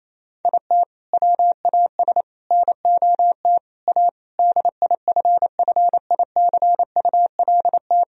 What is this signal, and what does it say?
Morse code, heard as a keyed tone, about 28 wpm, 700 Hz. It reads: IT WAS NOT A DIFFICULT
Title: Sentences from Top 1000 Words - Review Part 4 - 28wpm